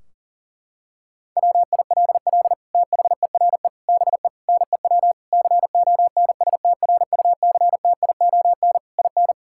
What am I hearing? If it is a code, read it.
Morse code, heard as a keyed tone, about 40 wpm, 700 Hz. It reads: WILL THERE BE NEW CONSTRUCTION IN
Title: Sentences from Top 1600 Words 40wpm